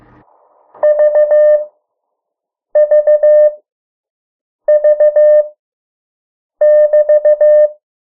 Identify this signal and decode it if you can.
Morse code, heard as a keyed tone, about 15 wpm, spaced as for 6 wpm, 600 Hz. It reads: VVV<BT>
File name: morse mp3 / 03 urok 2